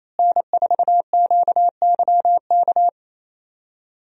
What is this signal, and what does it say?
Morse code, heard as a keyed tone, about 28 wpm, 700 Hz. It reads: N4QYK
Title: USA Call Signs - 5 Characters - Repeated 2x - 28wpm